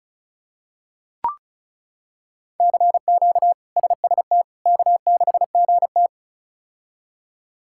Morse code, heard as a keyed tone, about 34 wpm, 700 Hz. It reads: CQ SST K6GT
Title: K1USN SST QSO Practice - 35wpm